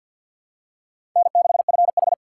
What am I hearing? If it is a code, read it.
Morse code, heard as a keyed tone, about 52 wpm, 700 Hz. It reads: N6FH
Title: USA Call Signs - 4 Characters - 2x Character Spacing - 50wpm